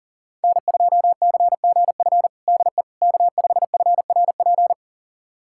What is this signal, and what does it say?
Morse code, heard as a keyed tone, about 40 wpm, 700 Hz. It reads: N2CGF DE K5FRP
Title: USA Callsign DE Callsign - 5 Characters - 40wpm